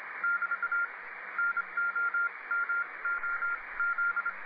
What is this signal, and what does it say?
Morse code, heard as a keyed tone, about 26 wpm, spaced as for 21 wpm, 1400 Hz, with radio noise. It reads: <BT> NOMOZ